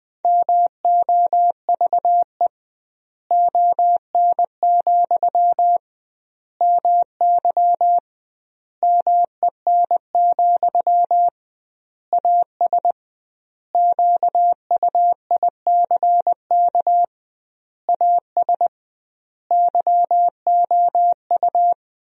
Morse code, heard as a keyed tone, about 20 wpm, 700 Hz. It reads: MOVE ON, MY MEN, AS QUICK AS YOU
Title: 2x Word-Spacing - Sentences from Top 500 Words - 20wpm